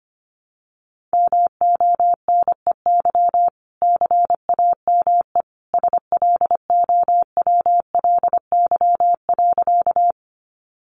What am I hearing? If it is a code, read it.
Morse code, heard as a keyed tone, about 25 wpm, 700 Hz. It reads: MONEY CAME SLOWLY.
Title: Sentences from Top 1100 Words 25wpm